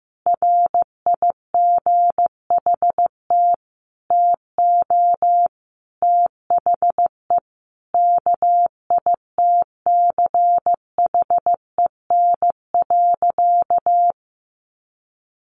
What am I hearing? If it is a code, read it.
Morse code, heard as a keyed tone, about 15 wpm, 700 Hz. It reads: RIGHT TO THE KITCHEN.
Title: Sentences from Top 1400 Words 15wpm